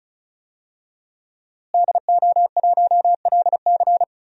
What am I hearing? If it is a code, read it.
Morse code, heard as a keyed tone, about 36 wpm, 700 Hz. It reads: DO1LC
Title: DXCC-100 Call Signs - 5 Characters - Repeated 2x - 35wpm